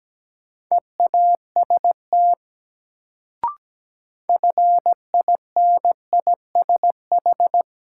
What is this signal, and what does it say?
Morse code, heard as a keyed tone, about 17 wpm, 700 Hz. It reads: EAST FINISH
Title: Sentences from Top 1200 Words - New Word Drill Alt - 17wpm